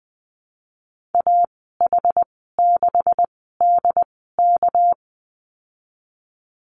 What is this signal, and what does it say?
Morse code, heard as a keyed tone, about 21 wpm, spaced as for 15 wpm, 700 Hz. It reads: AH6DK
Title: 238 Lesson 38g - Letter q - Call Signs 20wpm